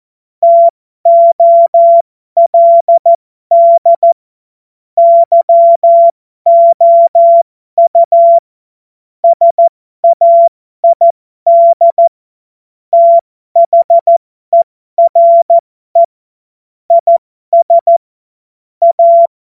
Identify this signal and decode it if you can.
Morse code, heard as a keyed tone, about 14 wpm, spaced as for 12 wpm, 680 Hz. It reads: TOLD YOU SAID THERE IS A